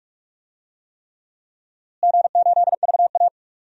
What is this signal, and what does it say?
Morse code, heard as a keyed tone, about 46 wpm, 700 Hz. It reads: G8VA
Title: DXCC-100 Call Signs - 4 Characters - Repeated 3x - 45wpm